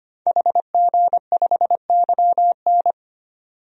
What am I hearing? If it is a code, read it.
Morse code, heard as a keyed tone, about 25 wpm, 700 Hz. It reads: HG5YN
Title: DXCC-100 Call Signs - 5 Characters - 25wpm